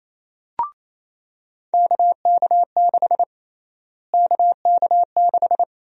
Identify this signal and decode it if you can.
Morse code, heard as a keyed tone, about 28 wpm, 700 Hz. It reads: KK6 KK6
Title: USA Call Sign Prefixes - Repeated 2x - 28wpm